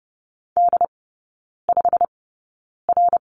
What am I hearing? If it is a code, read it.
Morse code, heard as a keyed tone, about 30 wpm, spaced as for 9 wpm, 700 Hz. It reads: D5R
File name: Lesson 15e - Number 5 - Three Characters 30wpm